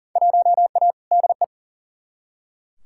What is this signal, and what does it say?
Morse code, heard as a keyed tone, about 40 wpm, 700 Hz. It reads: 1A DE
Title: Field Day Exchanges Encore - 40wpm